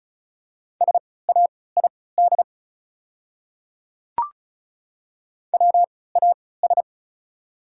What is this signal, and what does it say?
Morse code, heard as a keyed tone, about 36 wpm, spaced as for 20 wpm, 690 Hz. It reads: SAID WAS
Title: 3x Character Spacing - Top 100 Words - 35wpm